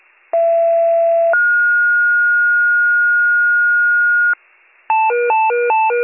Two-tone page, heard as a one-second tone, then a three-second tone, about 670 then 1430 Hz. It is heard from 0.3 to 4.3 s.